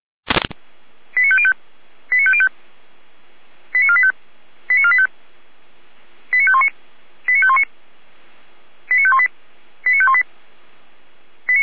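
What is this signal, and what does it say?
Five-tone selective call (ZVEI), heard as quick runs of five tones, tones of about 70 ms. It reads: 89505 89505 89485 89485 89519 89519 89518 89518
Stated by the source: ZVEI five-tone selective call received by SDR.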